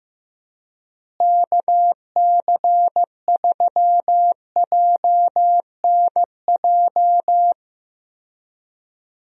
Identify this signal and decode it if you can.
Morse code, heard as a keyed tone, about 15 wpm, 700 Hz. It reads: KC3JNJ
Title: USA Call Signs - 6 Characters - Repeated 2x - 15wpm